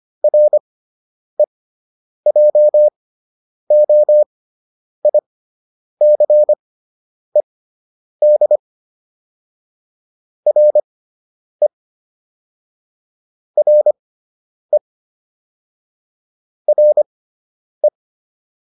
Morse code, heard as a keyed tone, about 25 wpm, spaced as for 9 wpm, 600 Hz. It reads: REJOICED RE RE RE